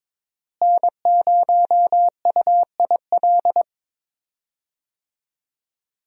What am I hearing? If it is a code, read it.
Morse code, heard as a keyed tone, about 22 wpm, 700 Hz. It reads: N0UIL
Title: USA Call Signs - 5 Characters - Repeated 2x - 22wpm